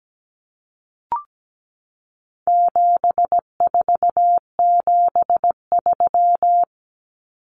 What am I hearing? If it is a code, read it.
Morse code, heard as a keyed tone, about 17 wpm, 700 Hz. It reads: 7473
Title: Common Cut Numbers - 4-Digit 17wpm